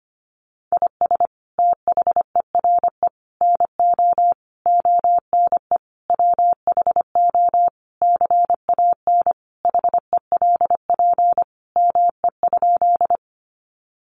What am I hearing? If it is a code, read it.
Morse code, heard as a keyed tone, about 25 wpm, 700 Hz. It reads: IS THERE NO ONE WHO CAN HELP ME?
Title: Sentences from Top 200 Words 25wpm